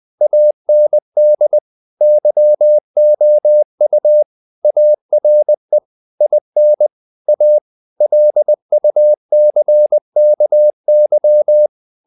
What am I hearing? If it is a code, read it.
Morse code, heard as a keyed tone, about 20 wpm, 600 Hz. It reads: AND YOU ARE IN A LUCKY
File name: the old man and the sea / the old man and the sea part 1 20wpm